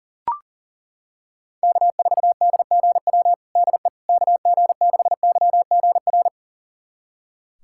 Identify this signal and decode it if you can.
Morse code, heard as a keyed tone, about 40 wpm, 700 Hz. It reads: K4DGW DE KG6YGR
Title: USA Callsign DE Callsign 40wpm